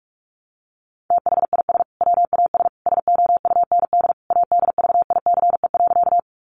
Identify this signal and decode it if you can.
Morse code, heard as a keyed tone, about 44 wpm, 700 Hz. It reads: THIS WAS SOUND ADVICE.
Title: Sentences from Top 1100 Words 45wpm